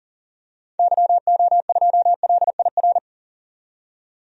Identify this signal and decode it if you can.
Morse code, heard as a keyed tone, about 40 wpm, 700 Hz. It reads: YO2LIR
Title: DXCC-100 Call Signs - 6 Characters - 40wpm